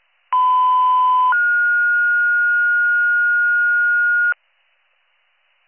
Two-tone page, heard as a one-second tone, then a three-second tone, about 1010 then 1430 Hz, with noise. It is heard from 0.3 to 4.3 s.